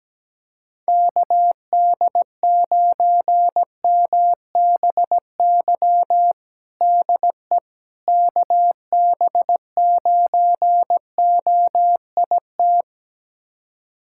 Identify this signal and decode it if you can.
Morse code, heard as a keyed tone, about 17 wpm, 700 Hz. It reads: KD9MBY DE KB9OIT
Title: USA Callsign DE Callsign 17wpm